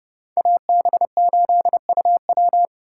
Morse code, heard as a keyed tone, about 28 wpm, 700 Hz. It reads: AB8UW